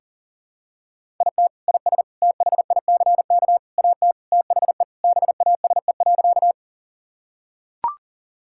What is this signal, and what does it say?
Morse code, heard as a keyed tone, about 40 wpm, 700 Hz. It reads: IT IS THICK AT THE BASE.
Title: Sentences from Top 1600 Words 40wpm